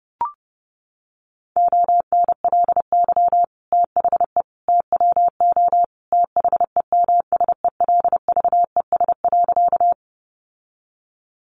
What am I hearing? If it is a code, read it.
Morse code, heard as a keyed tone, about 30 wpm, 700 Hz. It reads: ONLY THE TWO THEMSELVES.
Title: Sentences from Top 800 Words 30wpm